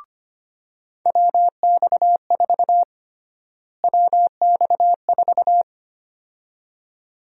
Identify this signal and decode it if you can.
Morse code, heard as a keyed tone, about 25 wpm, 700 Hz. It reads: WX4 WX4
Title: USA Call Sign Prefixes - Repeated 2x - 25wpm